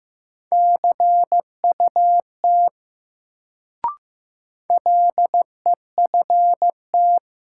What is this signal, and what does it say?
Morse code, heard as a keyed tone, about 15 wpm, 700 Hz. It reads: CUT LEFT
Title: Sentences from Top 300 Words - New Word Drill - 15wpm